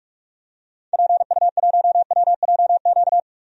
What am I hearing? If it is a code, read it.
Morse code, heard as a keyed tone, about 44 wpm, 700 Hz. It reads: PU1WJQ
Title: DXCC-100 Call Signs - 6 Characters - 45wpm